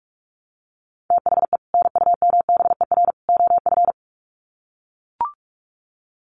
Morse code, heard as a keyed tone, about 44 wpm, 700 Hz. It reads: THE NUMBER OF
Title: Sets of 3 Words 45wpm